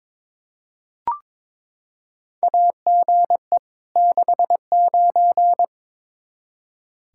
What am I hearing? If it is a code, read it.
Morse code, heard as a keyed tone, about 21 wpm, 700 Hz. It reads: AGE 69
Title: Practice QSO - Exchange 3 - More Info - 22wpm